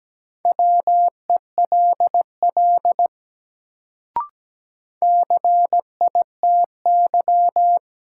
Morse code, heard as a keyed tone, about 17 wpm, 700 Hz. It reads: WELL CITY
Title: Sets of 2 Words - New Word Drill - 17wpm